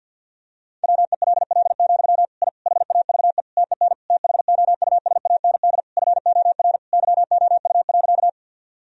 Morse code, heard as a keyed tone, about 51 wpm, 700 Hz. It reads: WELL, I HAVE TEN THOUSAND FOR YOU.
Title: Sentences from Top 1000 Words - Review Part 3 - 50wpm